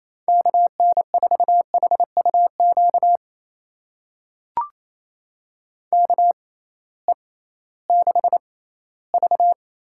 Morse code, heard as a keyed tone, about 28 wpm, spaced as for 16 wpm, 700 Hz. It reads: KN4HUQ KE6V
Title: USA Call Signs - 6 Characters - 6x Character Spacing - 28wpm